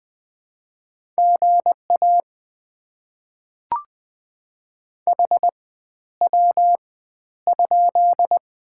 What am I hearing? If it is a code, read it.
Morse code, heard as a keyed tone, about 20 wpm, spaced as for 11 wpm, 690 Hz. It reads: GA HW?
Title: Top 90 QSO Elements - 4x Character Spacing - 20wpm